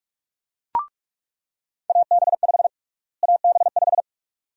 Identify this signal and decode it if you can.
Morse code, heard as a keyed tone, about 46 wpm, 700 Hz. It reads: AB5 AB5
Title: USA Call Sign Prefixes - Repeated 2x - 45wpm